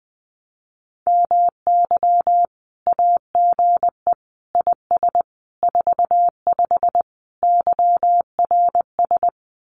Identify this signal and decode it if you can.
Morse code, heard as a keyed tone, about 20 wpm, 700 Hz. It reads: MY AGE IS 45 YRS
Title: Common QSO Phrases Encore 20wpm